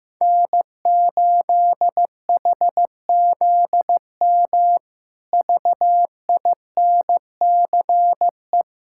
Morse code, heard as a keyed tone, about 15 wpm, 700 Hz. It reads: N8HZM VINCE
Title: K1USN SST QSO Practice - 15wpm